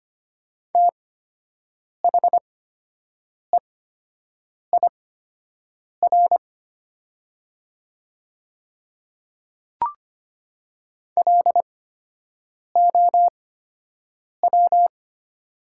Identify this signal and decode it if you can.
Morse code, heard as a keyed tone, about 25 wpm, spaced as for 7 wpm, 700 Hz. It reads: THEIR LOW
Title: 8x Character Spacing - Top 200 Words - 25wpm